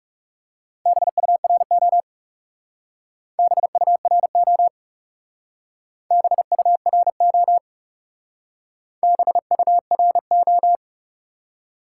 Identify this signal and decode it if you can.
Morse code, heard as a keyed tone, about 38 wpm, 700 Hz. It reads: BURO BURO BURO BURO